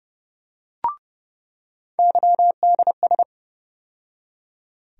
Morse code, heard as a keyed tone, about 32 wpm, 700 Hz. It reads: YDS
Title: USA Call Sign Postfixes - 3 Letters 30wpm